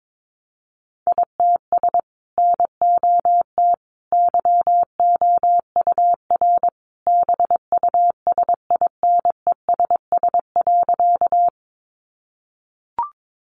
Morse code, heard as a keyed tone, about 22 wpm, 700 Hz. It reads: ITS NOT YOUR BUSINESS.
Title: Sentences from Top 600 Words 22wpm